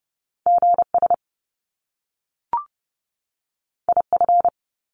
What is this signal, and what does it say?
Morse code, heard as a keyed tone, about 30 wpm, 700 Hz. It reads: GS IF
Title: Top Two Letter Combinations in Words 30wpm